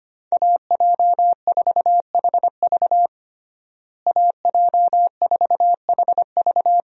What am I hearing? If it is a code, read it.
Morse code, heard as a keyed tone, about 25 wpm, 700 Hz. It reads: AJ4HV AJ4HV
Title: USA Call Signs - 5 Characters - Repeated 3x - 25wpm